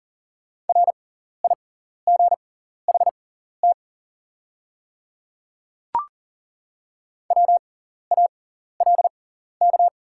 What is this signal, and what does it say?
Morse code, heard as a keyed tone, about 38 wpm, spaced as for 14 wpm, 690 Hz. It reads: RIGHT WALK
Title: 6x Character Spacing - Top 300 Words - 40wpm